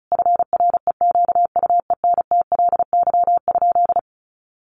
Morse code, heard as a keyed tone, about 34 wpm, 700 Hz. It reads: FREQUENTLY?
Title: Sentences from Top 1300 Words 35wpm